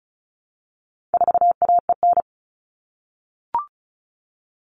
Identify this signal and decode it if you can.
Morse code, heard as a keyed tone, about 34 wpm, 700 Hz. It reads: 4AEN
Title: All Cut Numbers - 4-Digit 35wpm